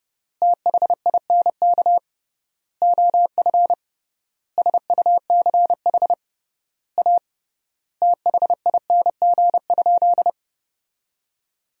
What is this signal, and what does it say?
Morse code, heard as a keyed tone, about 31 wpm, 700 Hz. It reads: THINK OF SUCH A THING?
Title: 3x Word-Spacing - Sentences from Top 200 Words - 30wpm